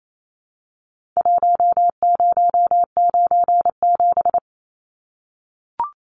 Morse code, heard as a keyed tone, about 28 wpm, 700 Hz. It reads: 1097
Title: Numbers - Contest Style - 4-Digit 28wpm